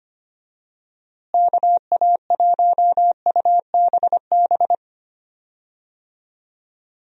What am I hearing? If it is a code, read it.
Morse code, heard as a keyed tone, about 25 wpm, 700 Hz. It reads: KA1UBB
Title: USA Call Signs - 6 Characters - Repeated 2x - 25wpm